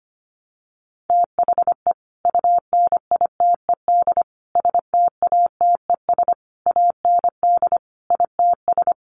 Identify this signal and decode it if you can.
Morse code, heard as a keyed tone, about 25 wpm, 700 Hz. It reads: THE UNITED STATES AND ITS
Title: Sets of 5 Words Encore 25wpm